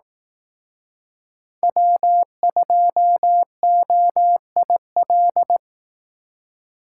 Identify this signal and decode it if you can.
Morse code, heard as a keyed tone, about 18 wpm, 700 Hz. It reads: W2OIL
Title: Speed Racing - USA 1x3 Call Signs 15wpm